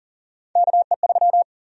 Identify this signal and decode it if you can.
Morse code, heard as a keyed tone, about 40 wpm, 700 Hz. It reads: KE3
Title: USA Call Sign Prefixes - Repeated 2x - 40wpm